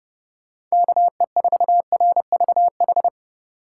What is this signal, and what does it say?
Morse code, heard as a keyed tone, about 36 wpm, 700 Hz. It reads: KE4RVH
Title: Speed Racing - USA 2x3 Call Signs 30wpm